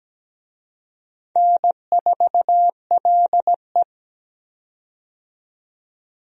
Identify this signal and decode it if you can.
Morse code, heard as a keyed tone, about 17 wpm, 700 Hz. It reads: N4LE
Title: USA Call Signs - 4 Characters - 17wpm